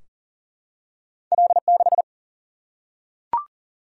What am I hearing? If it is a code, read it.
Morse code, heard as a keyed tone, about 42 wpm, 700 Hz. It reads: L6
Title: Single-Letter Single-Number 40wpm